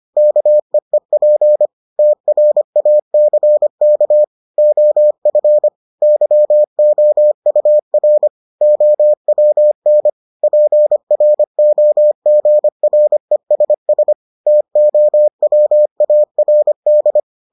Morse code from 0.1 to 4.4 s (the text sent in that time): KEEP TRACK